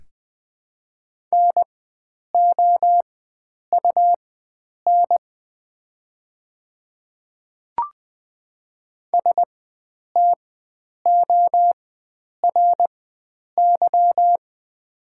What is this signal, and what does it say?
Morse code, heard as a keyed tone, about 21 wpm, spaced as for 9 wpm, 700 Hz. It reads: NOUN STORY
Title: 4x Character Spacing - Top 500 Words - 20wpm